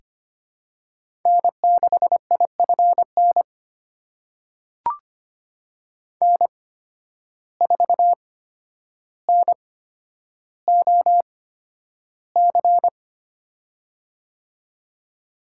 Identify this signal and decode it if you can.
Morse code, heard as a keyed tone, about 25 wpm, spaced as for 11 wpm, 700 Hz. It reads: N6IFN N4NOC
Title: USA Call Signs - 5 Characters - 8x Character Spacing - 25wpm